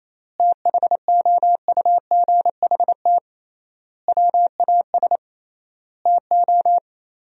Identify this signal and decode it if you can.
Morse code, heard as a keyed tone, about 28 wpm, 700 Hz. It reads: THOUGHT WAS TO
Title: 3x Word-Spacing - Sentences from Top 400 Words - 28wpm